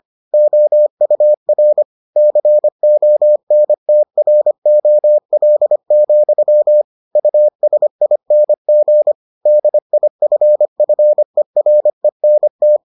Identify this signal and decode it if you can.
Morse code, heard as a keyed tone, about 25 wpm, 600 Hz. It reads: OUR CONTROL, USING DIFFERENT